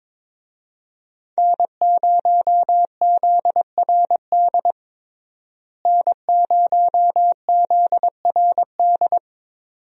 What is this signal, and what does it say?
Morse code, heard as a keyed tone, about 22 wpm, 700 Hz. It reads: N0ZRD N0ZRD
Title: USA Call Signs - 5 Characters - Repeated 2x - 22wpm